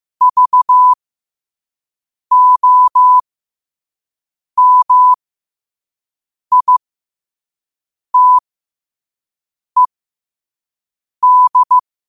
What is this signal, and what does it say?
Morse code, heard as a keyed tone, about 15 wpm, spaced as for 5 wpm, 1000 Hz. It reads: VOMITED
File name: qotdfiles / 5wpm